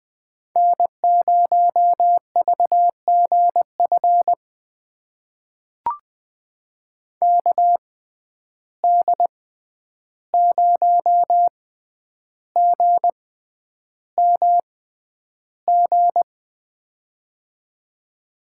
Morse code, heard as a keyed tone, about 20 wpm, spaced as for 10 wpm, 700 Hz. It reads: N0VGF KD0GMG
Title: USA Call Signs - 6x Character Spacing - 20wpm